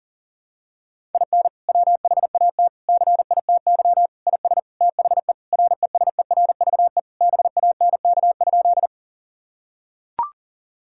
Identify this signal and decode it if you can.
Morse code, heard as a keyed tone, about 37 wpm, 700 Hz. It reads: IN WHAT CITY IS THE RESERVE BANK?
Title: Sentences from Top 1000 Words - Review Part 6 - 40wpm